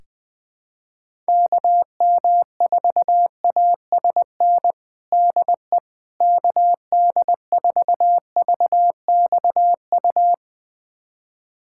Morse code, heard as a keyed tone, about 21 wpm, 700 Hz. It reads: KM4ASN DE KD4VXU
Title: USA Callsign DE Callsign - 6 Characters - 20wpm